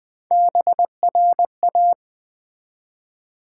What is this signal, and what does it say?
Morse code, heard as a keyed tone, about 20 wpm, 700 Hz. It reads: BRA
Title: Build A Word - 1 Letter At A Time - 3 to 5-Letter Words - 20wpm